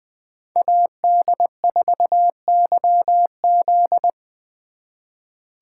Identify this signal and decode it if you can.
Morse code, heard as a keyed tone, about 20 wpm, 700 Hz. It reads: AD4YZ
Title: USA Call Signs - 5 Characters - 2x Character Spacing - 20wpm